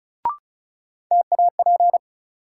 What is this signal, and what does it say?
Morse code, heard as a keyed tone, about 36 wpm, 700 Hz. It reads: TAP
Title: Sentences from Top 1400 Words - New Word Drill - 35wpm